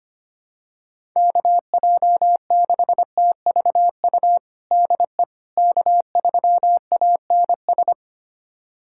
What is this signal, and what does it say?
Morse code, heard as a keyed tone, about 25 wpm, 700 Hz. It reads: KJ6TVU DE K3ANS
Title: USA Callsign DE Callsign 25wpm